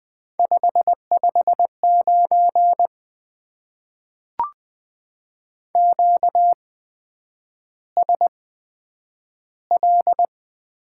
Morse code, heard as a keyed tone, about 20 wpm, spaced as for 9 wpm, 700 Hz. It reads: 559 QSL
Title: Top 90 QSO Elements - 8x Character Spacing - 20wpm